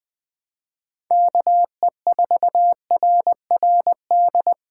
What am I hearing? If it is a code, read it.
Morse code, heard as a keyed tone, about 20 wpm, 700 Hz. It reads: KE4RRD